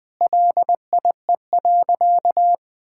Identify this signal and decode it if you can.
Morse code, heard as a keyed tone, about 20 wpm, 700 Hz. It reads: LIE.
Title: Sentences from Top 1600 Words 20wpm